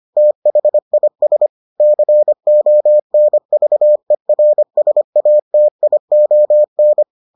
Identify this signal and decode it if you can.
Morse code, heard as a keyed tone, about 25 wpm, 600 Hz. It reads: THIS CONVERSATION